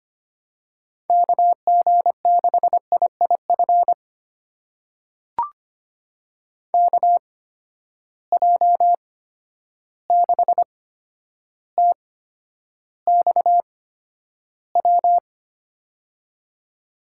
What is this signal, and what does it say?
Morse code, heard as a keyed tone, about 25 wpm, spaced as for 11 wpm, 700 Hz. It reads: KG6IIF KJ6TXW
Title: USA Call Signs - 6 Characters - 8x Character Spacing - 25wpm